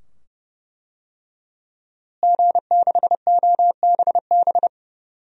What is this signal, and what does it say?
Morse code, heard as a keyed tone, about 31 wpm, 700 Hz. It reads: G6OBB
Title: DXCC-100 Call Signs - 5 Characters - Repeated 2x - 30wpm